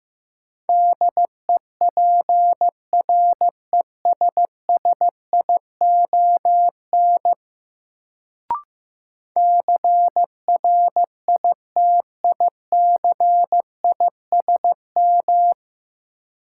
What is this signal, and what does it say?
Morse code, heard as a keyed tone, about 15 wpm, 700 Hz. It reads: DEPRESSION CRITICISM
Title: Sentences from Top 1300 Words - New Word Drill - 15wpm